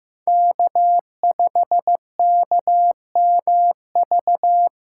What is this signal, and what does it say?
Morse code, heard as a keyed tone, about 15 wpm, 700 Hz. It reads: K5KMV
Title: USA Call Signs - 3x Character Spacing - 15wpm